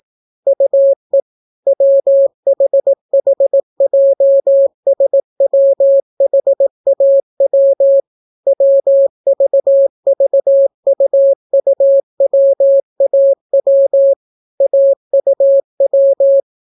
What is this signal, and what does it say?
Morse code, heard as a keyed tone, about 18 wpm, 550 Hz. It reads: UE WHHJSWHAW WVVUUWAW AUW